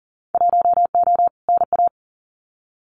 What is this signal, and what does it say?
Morse code, heard as a keyed tone, about 38 wpm, 700 Hz. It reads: 1O NA
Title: Winter Field Day Exchanges 40wpm